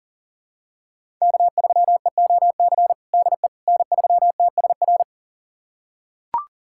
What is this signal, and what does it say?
Morse code, heard as a keyed tone, about 40 wpm, 700 Hz. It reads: K3EOC DE N3TSR
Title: USA Callsign DE Callsign - 5 Characters - 40wpm